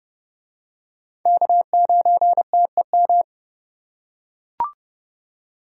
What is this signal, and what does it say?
Morse code, heard as a keyed tone, about 29 wpm, 700 Hz. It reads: K9TEM